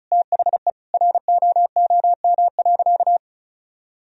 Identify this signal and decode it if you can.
Morse code, heard as a keyed tone, about 34 wpm, 700 Hz. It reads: THE ROOM.
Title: Sentences from Top 1000 Words - Review Part 4 - 35wpm